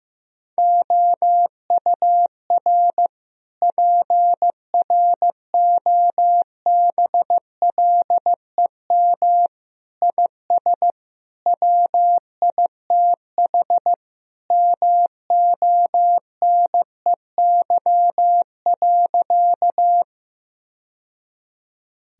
Morse code, heard as a keyed tone, about 15 wpm, 700 Hz. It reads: OUR PROBLEM IS WITH MONEY.
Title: Sentences from Top 1000 Words - Review Part 2 - 15wpm